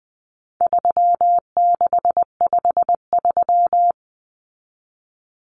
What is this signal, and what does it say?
Morse code, heard as a keyed tone, about 20 wpm, 700 Hz. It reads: 3653
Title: Common Cut Numbers - 4-Digit 20wpm